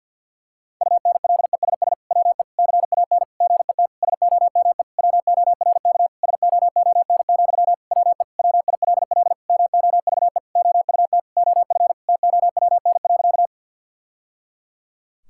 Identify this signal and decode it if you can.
Morse code, heard as a keyed tone, about 49 wpm, 700 Hz. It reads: UNLESS WE CAN GET SOME WORK SOON, WE WILL MOVE OUT OF TOWN.
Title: Sentences from Top 1000 Words - Review Part 6 - 50wpm